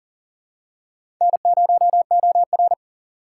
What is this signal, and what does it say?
Morse code, heard as a keyed tone, about 40 wpm, 700 Hz. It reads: N0OR
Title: USA Call Signs - 6x Character Spacing - 40wpm